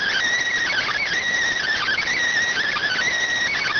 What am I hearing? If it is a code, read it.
Emergency Alert System (SAME) tones, heard as harsh buzzing bursts. It is heard from 0.7 to 3.2 s.